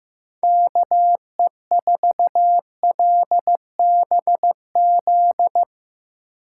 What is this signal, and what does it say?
Morse code, heard as a keyed tone, about 15 wpm, 700 Hz. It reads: KE4LBZ
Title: USA Call Signs - 6 Characters - 2x Character Spacing - 15wpm